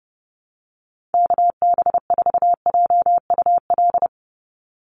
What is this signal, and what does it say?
Morse code, heard as a keyed tone, about 30 wpm, 700 Hz. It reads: KB4JUL
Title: USA Call Signs 30wpm